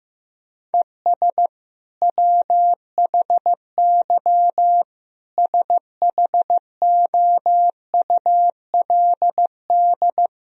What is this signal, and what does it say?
Morse code, heard as a keyed tone, about 16 wpm, 700 Hz. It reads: ES WHY SHOULD
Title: Sentences from Top 300 Words and Abbreviations 15wpm